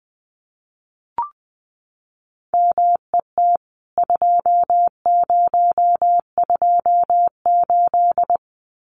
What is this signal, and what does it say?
Morse code, heard as a keyed tone, about 20 wpm, 700 Hz. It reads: MET 2028
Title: CWT Exchange 20wpm